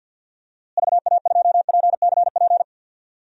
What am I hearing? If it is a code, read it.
Morse code, heard as a keyed tone, about 49 wpm, 700 Hz. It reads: VA2PCP
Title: DXCC-100 Call Signs - 6 Characters - Repeated 3x - 50wpm